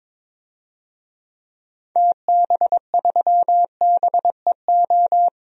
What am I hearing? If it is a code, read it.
Morse code, heard as a keyed tone, about 21 wpm, 700 Hz. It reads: TB3BEO
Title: DXCC-100 Call Signs - 6 Characters - Repeated 2x - 22wpm